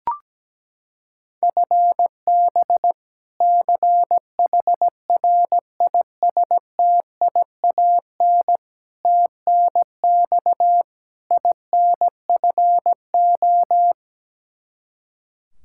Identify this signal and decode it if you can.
Morse code, heard as a keyed tone, about 17 wpm, 700 Hz. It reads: FB CHRISTIAN TNX INFO
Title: K1USN SST QSO Practice 17wpm